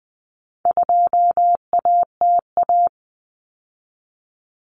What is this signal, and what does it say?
Morse code, heard as a keyed tone, about 20 wpm, 700 Hz. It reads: 2ATA
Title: Common Cut Numbers - 4-Digit 20wpm